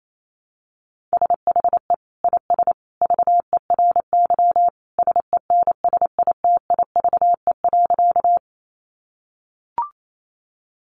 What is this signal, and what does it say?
Morse code, heard as a keyed tone, about 28 wpm, 700 Hz. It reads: SHE IS VERY SENSITIVE.
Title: Sentences from Top 1400 Words 28wpm